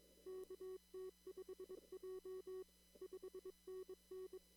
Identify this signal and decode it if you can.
Morse code, heard as a keyed tone, about 22 wpm, 380 Hz. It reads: KT5J 5NN
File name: CW / kt5j wpx cw-op-k5pi